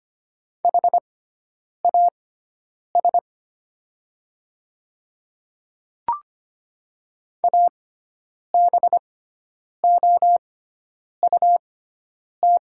Morse code, heard as a keyed tone, about 25 wpm, spaced as for 9 wpm, 690 Hz. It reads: HAS ABOUT
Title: 6x Character Spacing - Top 100 Words - 25wpm